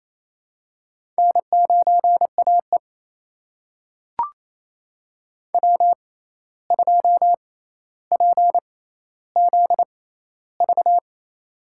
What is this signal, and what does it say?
Morse code, heard as a keyed tone, about 28 wpm, spaced as for 13 wpm, 700 Hz. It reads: N9AE W2PZV
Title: USA Call Signs - 6x Character Spacing - 28wpm